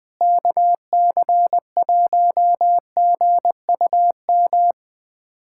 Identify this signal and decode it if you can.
Morse code, heard as a keyed tone, about 20 wpm, 700 Hz. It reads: KC1GUM
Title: USA Callsign DE Callsign - 6 Characters - 20wpm